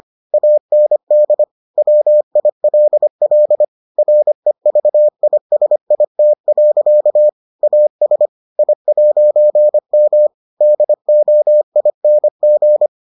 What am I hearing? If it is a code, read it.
Morse code, heard as a keyed tone, about 25 wpm, 600 Hz. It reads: AND WILL REVISIT. AS I'M DOING